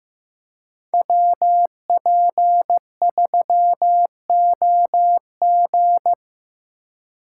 Morse code, heard as a keyed tone, about 15 wpm, 700 Hz. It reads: WP3OG
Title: USA Call Signs - 5 Characters - 2x Character Spacing - 15wpm